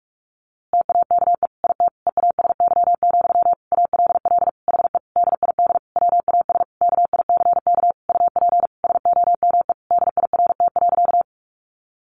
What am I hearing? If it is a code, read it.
Morse code, heard as a keyed tone, about 44 wpm, 700 Hz. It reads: TAKE IT EASY, ALL HE DID WAS KICK UP SOME DIRT.
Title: A Sound Of Thunder 45wpm